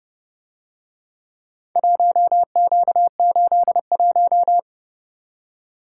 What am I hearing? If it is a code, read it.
Morse code, heard as a keyed tone, about 28 wpm, 700 Hz. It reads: 1Q81